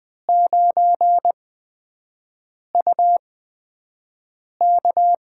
Morse code, heard as a keyed tone, about 20 wpm, spaced as for 5 wpm, 700 Hz. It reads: 9UK
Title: USA Call Signs - 5 Characters - 8x Character Spacing - 20wpm